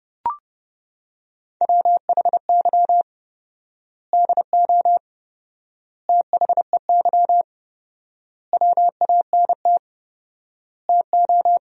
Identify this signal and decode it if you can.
Morse code, heard as a keyed tone, about 30 wpm, 700 Hz. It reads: WHY DO THEY WANT TO
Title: 4x Word-Spacing - Sentences from Top 400 Words - 30wpm